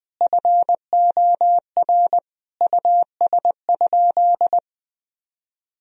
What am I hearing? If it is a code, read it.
Morse code, heard as a keyed tone, about 21 wpm, 700 Hz. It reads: FOR US?
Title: Sentences from Top 1000 Words - Review Part 2 - 20wpm